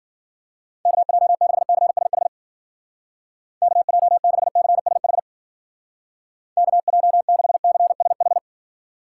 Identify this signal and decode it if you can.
Morse code, heard as a keyed tone, about 49 wpm, 700 Hz. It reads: KJ6CSH KJ6CSH KJ6CSH
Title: Speed Racing - USA 2x3 Call Signs 40wpm